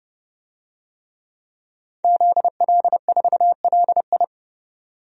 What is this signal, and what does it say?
Morse code, heard as a keyed tone, about 30 wpm, 700 Hz. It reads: ZL4LI
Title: DXCC-100 Call Signs - 5 Characters - 30wpm